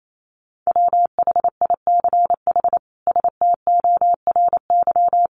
Morse code, heard as a keyed tone, about 28 wpm, 700 Hz. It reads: WHICH STORY
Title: Sentences from Top 300 Words 28wpm